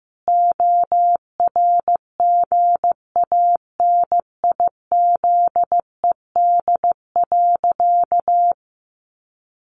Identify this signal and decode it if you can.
Morse code, heard as a keyed tone, about 15 wpm, 700 Hz. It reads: ORGANIZED.
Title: Sentences from Top 1300 Words 15wpm